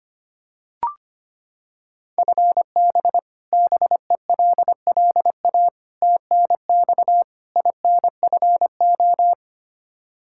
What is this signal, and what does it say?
Morse code, heard as a keyed tone, about 25 wpm, 700 Hz. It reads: FB BELLA TNX INFO
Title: Practice QSO - Exchange 3 - More Info - 25wpm